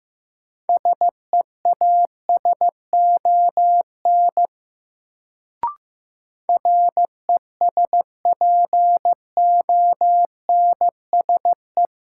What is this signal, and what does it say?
Morse code, heard as a keyed tone, about 15 wpm, 700 Hz. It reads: SEASON RESPONSE
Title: Sentences from Top 1400 Words - New Word Drill - 15wpm